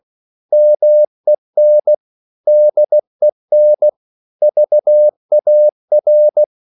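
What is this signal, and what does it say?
Morse code, heard as a keyed tone, about 16 wpm, 600 Hz. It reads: MEN DEN VAR
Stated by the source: Slow (QRS) Morse code at 16 wpm, from CW med Gnister.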